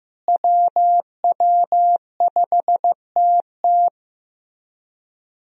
Morse code, heard as a keyed tone, about 16 wpm, 700 Hz. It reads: WW5TT
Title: USA Call Signs - 5 Characters - Repeated 3x - 15wpm